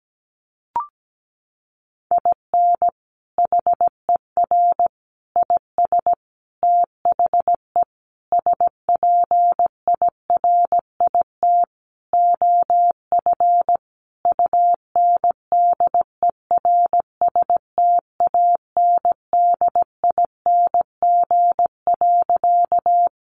Morse code, heard as a keyed tone, about 17 wpm, 700 Hz. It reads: IN HER IS THE SPIRIT OF UNDERSTANDING.
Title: Sentences from Top 1400 Words 17wpm